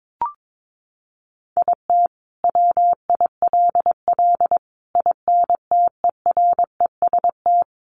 Morse code, heard as a keyed tone, about 22 wpm, 700 Hz. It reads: IT WILL INTEREST
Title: Sentences from Top 400 Words 22wpm